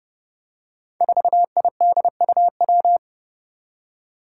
Morse code, heard as a keyed tone, about 32 wpm, 700 Hz. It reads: 4IDUW
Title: Five Challenging Letter-Number Combinations 30wpm